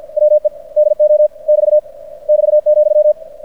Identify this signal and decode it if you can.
Morse code, heard as a keyed tone, about 50 wpm, 600 Hz. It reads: WE DO <BT> <BT>,
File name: archived / CRSnet